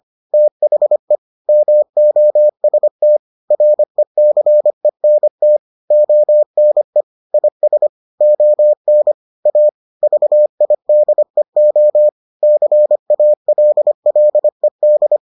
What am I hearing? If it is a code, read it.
Morse code, heard as a keyed tone, about 25 wpm, 600 Hz. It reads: THE MOST RECENT ONE IS ON A VIDEO CALLED